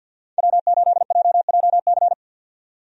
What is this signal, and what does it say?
Morse code, heard as a keyed tone, about 49 wpm, 700 Hz. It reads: W8JJC
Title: USA Call Signs - 5 Characters - 50wpm